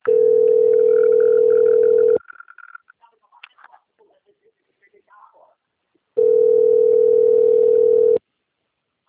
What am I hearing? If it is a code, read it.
Ringback tone (North America), 440 + 480 Hz.